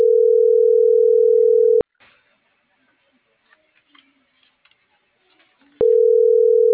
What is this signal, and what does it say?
Ringback tone (North America), 440 + 480 Hz.